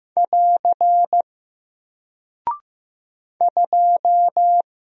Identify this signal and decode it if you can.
Morse code, heard as a keyed tone, about 15 wpm, 700 Hz. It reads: <AR> 2
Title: Single Character (Letter:Number:Punctuation:Prosign) 15wpm